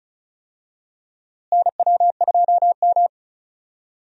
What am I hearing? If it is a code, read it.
Morse code, heard as a keyed tone, about 36 wpm, 700 Hz. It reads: NW2M
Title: USA Call Signs - 4 Characters - Repeated 2x - 35wpm